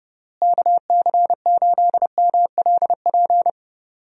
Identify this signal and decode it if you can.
Morse code, heard as a keyed tone, about 32 wpm, 700 Hz. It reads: KC8MLP
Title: USA Call Signs - 6 Characters - Repeated 3x - 30wpm